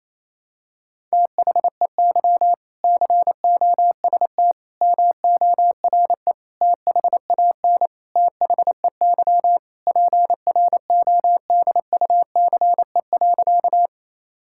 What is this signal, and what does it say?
Morse code, heard as a keyed tone, about 28 wpm, 700 Hz. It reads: THEY COST MORE THAN THEY PRODUCE.
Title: Sentences from Top 1000 Words - Review Part 5 - 28wpm